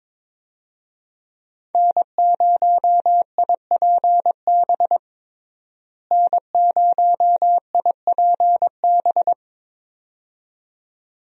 Morse code, heard as a keyed tone, about 21 wpm, 700 Hz. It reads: N0IPB N0IPB
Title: USA Call Signs - 5 Characters - Repeated 2x - 22wpm